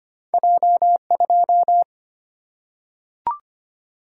Morse code, heard as a keyed tone, about 25 wpm, 700 Hz. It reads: J 2
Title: Single-Letter Single-Number 25wpm